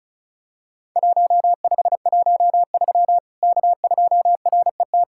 Morse code, heard as a keyed tone, about 36 wpm, 700 Hz. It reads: 1513 K2RET
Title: Worked All Europe DX Contest - QTC Exchange - 35wpm